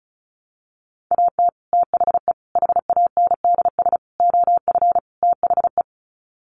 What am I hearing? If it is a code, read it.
Morse code, heard as a keyed tone, about 34 wpm, 700 Hz. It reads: AT THE HANDS OF THE
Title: Sets of 5 Words Encore2 35wpm